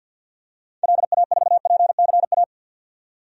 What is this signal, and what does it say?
Morse code, heard as a keyed tone, about 50 wpm, 700 Hz. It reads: LA4PCA
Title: DXCC-100 Call Signs - 6 Characters - 50wpm